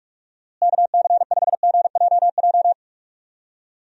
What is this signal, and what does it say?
Morse code, heard as a keyed tone, about 46 wpm, 700 Hz. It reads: KC5GJJ
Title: USA Call Signs - 6 Characters - Repeated 2x - 45wpm